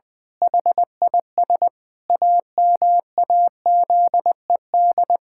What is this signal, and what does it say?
Morse code, heard as a keyed tone, about 20 wpm, 700 Hz. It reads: HIS AMAZED
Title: Pangram Challenge v2 - 20wpm